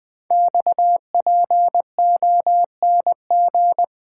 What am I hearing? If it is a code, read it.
Morse code, heard as a keyed tone, about 20 wpm, 700 Hz. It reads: XPONG
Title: Code Groups - 5 Letters - 20wpm